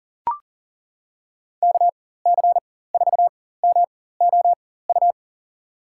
Morse code, heard as a keyed tone, about 37 wpm, spaced as for 18 wpm, 700 Hz. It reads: KC4MOU
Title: USA Call Signs - 6 Characters - 4x Character Spacing - 40wpm